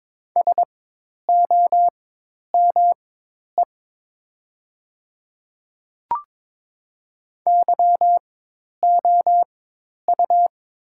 Morse code, heard as a keyed tone, about 22 wpm, spaced as for 10 wpm, 700 Hz. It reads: SOME YOU
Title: 4x Character Spacing - Top 200 Words - 22wpm